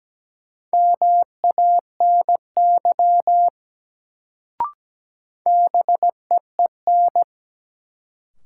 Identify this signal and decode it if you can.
Morse code, heard as a keyed tone, about 17 wpm, 700 Hz. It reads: MANY BEEN
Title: Sets of 2 Words - New Word Drill - 17wpm